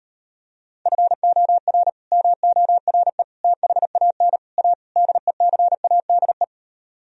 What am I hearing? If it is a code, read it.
Morse code, heard as a keyed tone, about 33 wpm, 700 Hz. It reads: FOR MORE THAN A DECADE